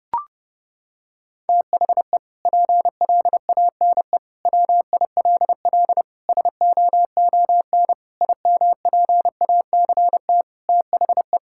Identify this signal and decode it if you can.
Morse code, heard as a keyed tone, about 30 wpm, 700 Hz. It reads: THE PLANE WILL SOON IMPACT THE
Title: Sentences from Top 1000 Words - Review Part 6 - 30wpm